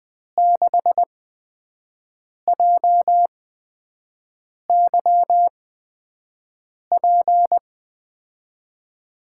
Morse code, heard as a keyed tone, about 20 wpm, spaced as for 5 wpm, 700 Hz. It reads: 6JYP
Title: USA Call Signs - 8x Character Spacing - 20wpm